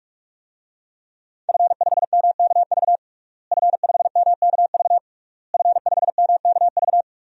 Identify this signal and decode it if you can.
Morse code, heard as a keyed tone, about 46 wpm, 700 Hz. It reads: F5MKV F5MKV F5MKV
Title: DXCC-100 Call Signs - 5 Characters - Repeated 3x - 45wpm